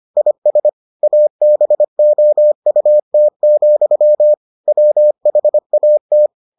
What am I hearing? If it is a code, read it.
Morse code, heard as a keyed tone, about 25 wpm, 600 Hz. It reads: IS ABOUT, WHAT